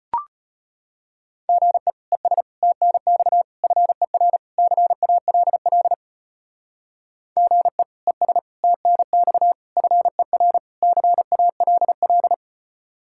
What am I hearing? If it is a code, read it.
Morse code, heard as a keyed tone, about 32 wpm, 700 Hz. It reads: GE ES TNX FER CALL GE ES TNX FER CALL